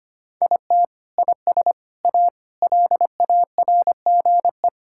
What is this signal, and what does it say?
Morse code, heard as a keyed tone, about 25 wpm, 700 Hz. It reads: IT IS A LARGE
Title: Sentences from Top 1000 Words - Review Part 6 - 25wpm